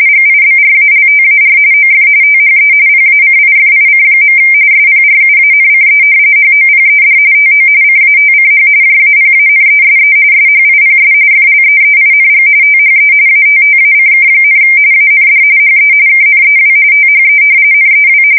RTTY, about 45.45 baud. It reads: GIBRALTAR OF THE PACIFIC. SPRUANCE HOPED TO CATCH JAPANESE SHIPS FLEEING THE ATTACK, AND HE KNEW THE NEW
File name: fsk170 / battleshipnj